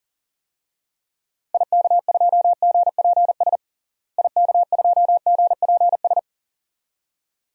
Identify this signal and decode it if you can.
Morse code, heard as a keyed tone, about 40 wpm, 700 Hz. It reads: IK2GPS IK2GPS
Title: DXCC-100 Call Signs - 6 Characters - Repeated 2x - 40wpm